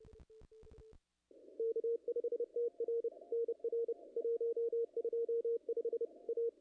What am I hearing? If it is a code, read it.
Morse code, heard as a keyed tone, about 29 wpm, 440 Hz. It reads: K5TR NR 125 A
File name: contests / ss-cw-k5tr-op-n5rz -0000